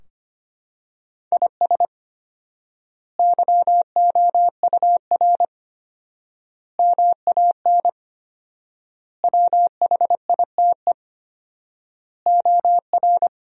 Morse code, heard as a keyed tone, about 25 wpm, 700 Hz. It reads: IS YOUR MAN WHITE OR
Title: 4x Word-Spacing - Sentences from Top 400 Words - 25wpm